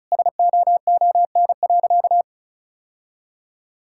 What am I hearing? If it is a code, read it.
Morse code, heard as a keyed tone, about 34 wpm, 700 Hz. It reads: SOON.
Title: Sentences from Top 1600 Words 35wpm